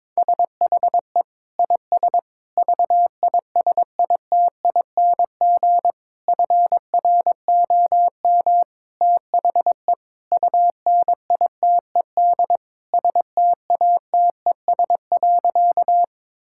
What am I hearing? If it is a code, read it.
Morse code, heard as a keyed tone, about 22 wpm, 700 Hz. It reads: SHE IS VISITING FROM THE UNITED STATES.
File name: practice / 22wpm / Sentences with 4-Word Phrases - 22wpm